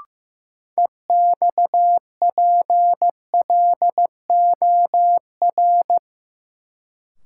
Morse code, heard as a keyed tone, about 15 wpm, 700 Hz. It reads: EXPLOR
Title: Sentences from Top 1100 Words - New Word Drill Alt - 15wpm